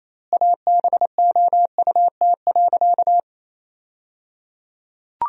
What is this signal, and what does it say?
Morse code, heard as a keyed tone, about 28 wpm, 700 Hz. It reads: ABOUT.
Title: 3x Word-Spacing - Sentences from Top 400 Words - 28wpm